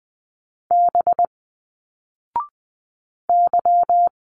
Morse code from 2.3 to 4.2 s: Y